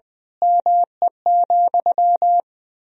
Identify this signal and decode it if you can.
Morse code, heard as a keyed tone, about 20 wpm, 700 Hz. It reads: ME,
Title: Sentences from Top 1000 Words - Review Part 3 - 20wpm